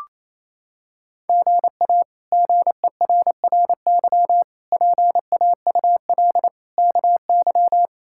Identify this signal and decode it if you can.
Morse code, heard as a keyed tone, about 28 wpm, 700 Hz. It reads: GA GERRY PAUL KY
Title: K1USN SST QSO Practice - 28wpm